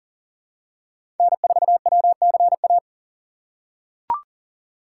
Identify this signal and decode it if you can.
Morse code, heard as a keyed tone, about 40 wpm, 700 Hz. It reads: N4WCA
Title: USA Call Signs - 5 Characters - 3x Character Spacing - 40wpm